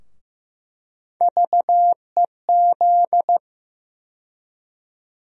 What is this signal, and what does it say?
Morse code, heard as a keyed tone, about 15 wpm, 700 Hz. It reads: VEZ
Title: USA Call Sign Postfixes - 3 Letters 15wpm